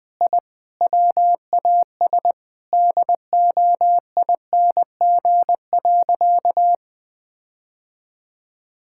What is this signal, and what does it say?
Morse code, heard as a keyed tone, about 21 wpm, 700 Hz. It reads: I WAS DOING.
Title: Sentences with 8-Word Phrases - Spoken Beginning - 20wpm